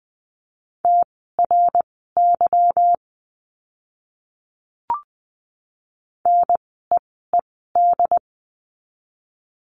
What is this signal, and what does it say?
Morse code, heard as a keyed tone, about 20 wpm, spaced as for 14 wpm, 700 Hz. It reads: TRY NEED
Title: 192 Lesson 31f - Letter v - Words 20wpm